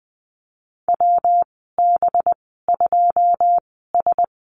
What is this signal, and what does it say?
Morse code, heard as a keyed tone, about 20 wpm, spaced as for 14 wpm, 700 Hz. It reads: WB2S
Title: 193 Lesson 31g - Letter v - Call Signs 20wpm